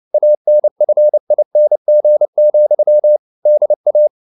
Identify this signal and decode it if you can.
Morse code, heard as a keyed tone, about 29 wpm, 600 Hz. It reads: ANFING, DA